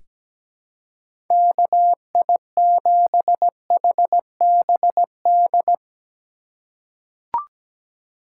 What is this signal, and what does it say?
Morse code, heard as a keyed tone, about 17 wpm, 700 Hz. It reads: KI7HBD
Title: USA Call Signs - 2x Character Spacing - 17wpm